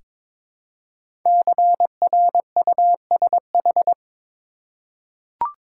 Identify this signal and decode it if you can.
Morse code, heard as a keyed tone, about 22 wpm, 700 Hz. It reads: CRUSH
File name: 5 Letter Words - First 3 Letters Given - 22wpm